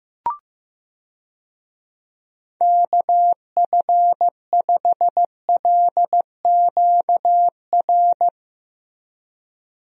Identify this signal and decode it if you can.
Morse code, heard as a keyed tone, about 15 wpm, 700 Hz. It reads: KF5LQR
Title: USA Call Signs - 6 Characters - Repeated 3x - 15wpm